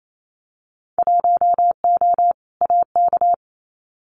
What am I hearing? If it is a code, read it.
Morse code, heard as a keyed tone, about 28 wpm, 690 Hz. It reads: 1O AK
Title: Winter Field Day Exchanges Encore 28wpm